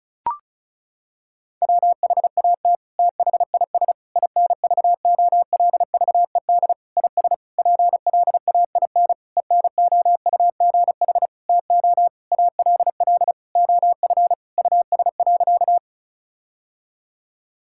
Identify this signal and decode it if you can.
Morse code, heard as a keyed tone, about 36 wpm, 700 Hz. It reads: WHAT THIS INVOLVED IS PLAIN ENOUGH TO ALL OF US.
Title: Sentences from Top 1000 Words - Review Part 5 - 35wpm